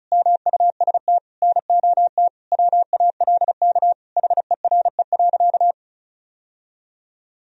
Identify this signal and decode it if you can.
Morse code, heard as a keyed tone, about 35 wpm, 700 Hz. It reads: MUST NOT WALK HERE.
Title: Sentences from Top 1000 Words - Review Part 1 - 35wpm